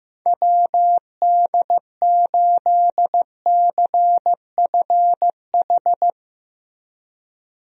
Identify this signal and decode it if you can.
Morse code, heard as a keyed tone, about 16 wpm, 700 Hz. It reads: WD8CFH
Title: USA Call Signs - 6 Characters - Repeated 2x - 15wpm